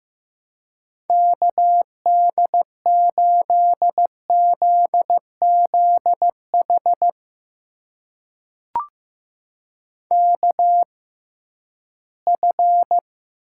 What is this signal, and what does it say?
Morse code, heard as a keyed tone, about 15 wpm, spaced as for 11 wpm, 700 Hz. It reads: KD8ZZH KF
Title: USA Call Signs - 6 Characters - 6x Character Spacing - 15wpm